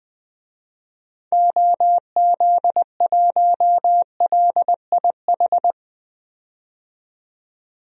Morse code, heard as a keyed tone, about 20 wpm, 700 Hz. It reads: OZ1LIH
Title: DXCC-100 Call Signs - 6 Characters - 20wpm